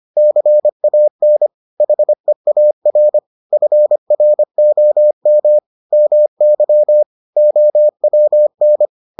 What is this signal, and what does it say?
Morse code, heard as a keyed tone, about 25 wpm, 600 Hz. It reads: CAN HEAR FROM MY OWN